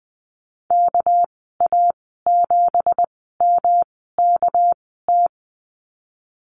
Morse code, heard as a keyed tone, about 19 wpm, spaced as for 14 wpm, 700 Hz. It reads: KA7MKT
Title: 231 Lesson 37g - Letter x - Call Signs 20wpm